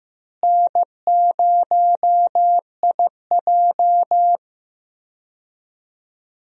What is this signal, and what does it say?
Morse code, heard as a keyed tone, about 15 wpm, 700 Hz. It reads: N0IJ
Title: USA Call Signs - 4 Characters - Repeated 3x - 15wpm